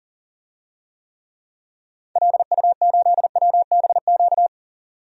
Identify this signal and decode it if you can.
Morse code, heard as a keyed tone, about 40 wpm, 700 Hz. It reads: LU8WBQ
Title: DXCC-100 Call Signs - 6 Characters - 40wpm